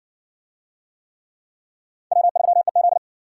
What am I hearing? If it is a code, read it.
Morse code, heard as a keyed tone, about 52 wpm, 700 Hz. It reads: W3EZ